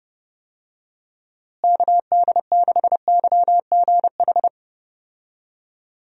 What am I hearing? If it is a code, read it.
Morse code, heard as a keyed tone, about 32 wpm, 700 Hz. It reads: KD6YGH
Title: USA Call Signs - 6 Characters - Repeated 2x - 30wpm